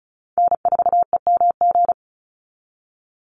Morse code, heard as a keyed tone, about 34 wpm, 700 Hz. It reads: N4EMG
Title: NA SOTA Chaser Callsigns 35wpm